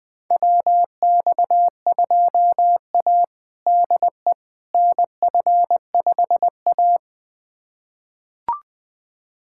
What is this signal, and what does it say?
Morse code, heard as a keyed tone, about 20 wpm, 700 Hz. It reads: WX2A DE NF5A
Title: USA Callsign DE Callsign - 4 Characters - 20wpm